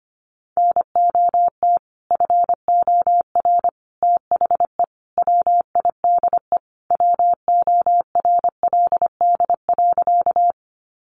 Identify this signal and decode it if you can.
Morse code, heard as a keyed tone, about 25 wpm, 700 Hz. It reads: NOT FOR THE WIDE WORLD.
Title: Sentences from Top 900 Words 25wpm